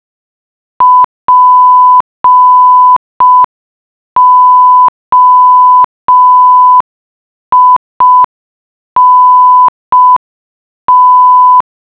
Morse code, heard as a keyed tone, about 5 wpm, 1000 Hz. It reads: POINT